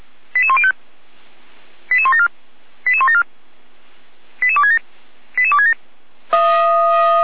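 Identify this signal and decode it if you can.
Five-tone selective call (ZVEI), heard as quick runs of five tones, tones of about 70 ms. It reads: 88105 88174 88174 88267 88267
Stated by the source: ZVEI five-tone selective call received by SDR.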